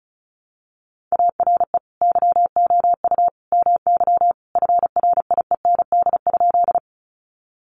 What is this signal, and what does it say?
Morse code, heard as a keyed tone, about 34 wpm, 700 Hz. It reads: ARE YOU MY FRIEND?
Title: Sentences from Top 300 Words 35wpm